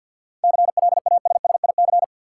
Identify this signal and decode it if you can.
Morse code, heard as a keyed tone, about 53 wpm, 700 Hz. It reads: CLASSIC
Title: Sentences from Top 1000 Words - New Word Drill - 50wpm